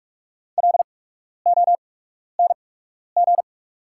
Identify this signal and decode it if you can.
Morse code, heard as a keyed tone, about 44 wpm, spaced as for 12 wpm, 690 Hz. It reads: LONG
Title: 8x Character Spacing - Top 200 Words - 45wpm